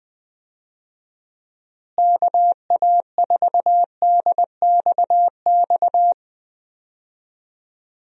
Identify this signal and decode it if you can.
Morse code, heard as a keyed tone, about 20 wpm, 700 Hz. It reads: KA4DXX
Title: USA Call Signs - 6 Characters - Repeated 3x - 20wpm